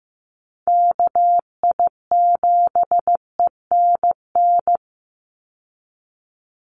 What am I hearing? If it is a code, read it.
Morse code, heard as a keyed tone, about 15 wpm, 700 Hz. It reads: KI7ENN
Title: USA Call Signs 15wpm